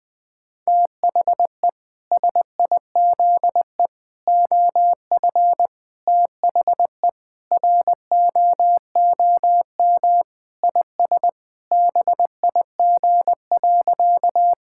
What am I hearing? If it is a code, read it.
Morse code, heard as a keyed tone, about 21 wpm, 700 Hz. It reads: THE SIZE OF THE ROOM IS BIG.
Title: Sentences with 3-Word Phrases - Spoken Beginning - 20wpm